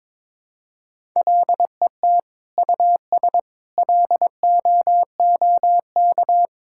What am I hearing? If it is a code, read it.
Morse code, heard as a keyed tone, about 21 wpm, 700 Hz. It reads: LET US LOOK
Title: Sentences from Top 1000 Words - Review Part 6 - 22wpm